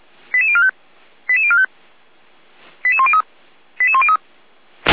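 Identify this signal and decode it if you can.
Five-tone selective call (ZVEI), heard as quick runs of five tones, tones of about 70 ms. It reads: 88045 88045 88103 88103
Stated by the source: ZVEI five-tone selective call 88231 received via SDR.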